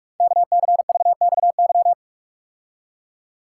Morse code, heard as a keyed tone, about 44 wpm, 700 Hz. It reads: KCVXY
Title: Code Groups - 5 Letters - 45wpm